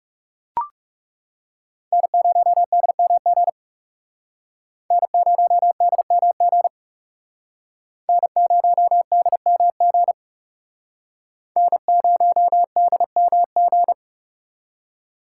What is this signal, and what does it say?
Morse code, heard as a keyed tone, about 39 wpm, 700 Hz. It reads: N0DMG N0DMG N0DMG N0DMG